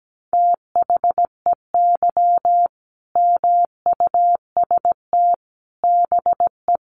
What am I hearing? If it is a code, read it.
Morse code, heard as a keyed tone, about 17 wpm, 700 Hz. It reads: THEY MUST BE
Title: Sentences from Top 1400 Words 17wpm